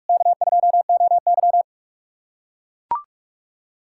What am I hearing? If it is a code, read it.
Morse code, heard as a keyed tone, about 41 wpm, 700 Hz. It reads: K2OY
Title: USA Callsign DE Callsign - 4 Characters - 45wpm